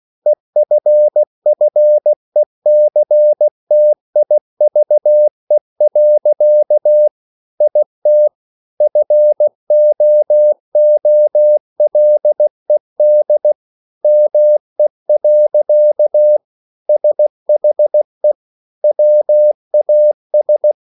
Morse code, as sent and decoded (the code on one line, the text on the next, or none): . ..-. ..-. -.-. - .. ...- . .-.-.- / .. - / ..-. --- --- .-.. -.. / -- . .-.-.- / ... .... / .-- .- ...
EFFECTIVE. IT FOOLED ME. SHE WAS